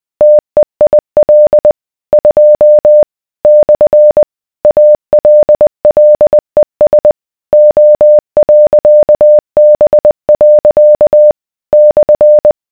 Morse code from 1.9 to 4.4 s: 2 /